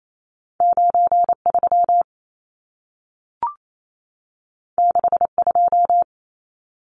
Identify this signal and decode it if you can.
Morse code, heard as a keyed tone, about 28 wpm, 700 Hz. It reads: 93 62
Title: Numbers - 2-Digit 28wpm